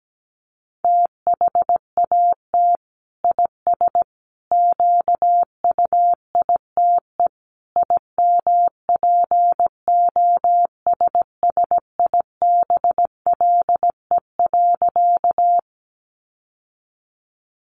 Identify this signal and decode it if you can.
Morse code, heard as a keyed tone, about 17 wpm, 700 Hz. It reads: THAT IS QUITE IMPOSSIBLE.
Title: Sentences from Top 1100 Words 17wpm